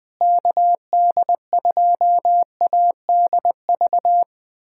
Morse code, heard as a keyed tone, about 21 wpm, 700 Hz. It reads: KD2ADV
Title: USA Callsign DE Callsign - 6 Characters - 20wpm